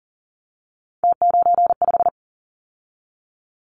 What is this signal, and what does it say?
Morse code, heard as a keyed tone, about 41 wpm, 700 Hz. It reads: T95
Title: Callsign Prefixes for Top 50 DXCC - 40wpm